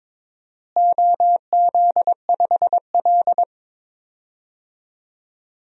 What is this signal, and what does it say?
Morse code, heard as a keyed tone, about 22 wpm, 700 Hz. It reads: OZ5L
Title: DXCC-100 Call Signs - 4 Characters - Repeated 2x - 22wpm